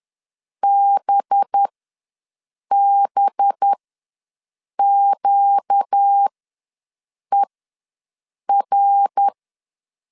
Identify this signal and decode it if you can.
Morse code, heard as a keyed tone, about 10 wpm, spaced as for 6 wpm, 800 Hz. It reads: BBQER
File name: LEZIONI / 5 LBVC / 5 LB ws 10wpm 3616